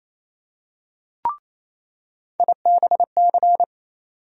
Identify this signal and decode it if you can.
Morse code, heard as a keyed tone, about 28 wpm, 700 Hz. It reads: IBC
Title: USA Call Sign Postfixes - 3 Letters 28wpm